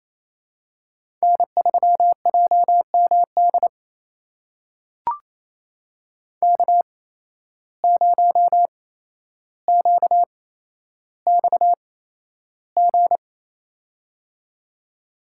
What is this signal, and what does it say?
Morse code, heard as a keyed tone, about 28 wpm, spaced as for 12 wpm, 700 Hz. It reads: N3JMD K0QXG
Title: USA Call Signs - 5 Characters - 8x Character Spacing - 28wpm